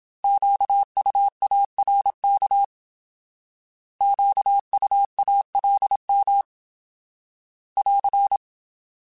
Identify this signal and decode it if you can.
Morse code, heard as a keyed tone, about 27 wpm, 800 Hz. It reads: QUARK QUALM <AR>
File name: Alphabetical / Alphabetical mp3s / Qs @25